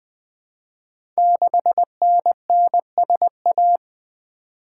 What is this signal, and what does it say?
Morse code, heard as a keyed tone, about 20 wpm, 700 Hz. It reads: 6NNSA
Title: Code Groups - 5 Letter:Numbers - 20wpm